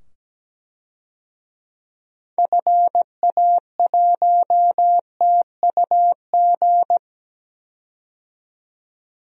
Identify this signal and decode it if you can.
Morse code, heard as a keyed tone, about 17 wpm, 700 Hz. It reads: FA1TUG
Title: DXCC-100 Call Signs - 6 Characters - 17wpm